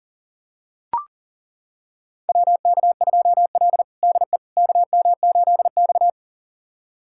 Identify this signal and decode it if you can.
Morse code, heard as a keyed tone, about 37 wpm, 700 Hz. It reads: WK2L DE KM8X
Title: USA Callsign DE Callsign - 4 Characters - 40wpm